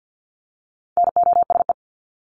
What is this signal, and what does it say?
Morse code, heard as a keyed tone, about 50 wpm, 700 Hz. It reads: NOSE
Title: Top 1501-1750 Words 50wpm